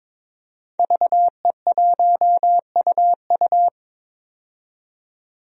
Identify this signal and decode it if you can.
Morse code, heard as a keyed tone, about 22 wpm, 700 Hz. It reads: VE1UU
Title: DXCC-100 Call Signs - 5 Characters - 22wpm